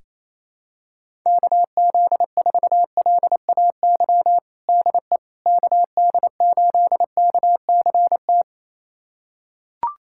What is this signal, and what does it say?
Morse code, heard as a keyed tone, about 28 wpm, 700 Hz. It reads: KZ4LAY DE KD8KCT
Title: USA Callsign DE Callsign - 6 Characters - 28wpm